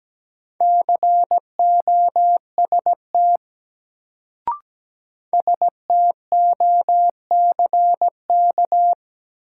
Morse code, heard as a keyed tone, about 17 wpm, 700 Hz. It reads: COST STOCK
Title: Sentences from Top 800 Words - New Word Drill - 17wpm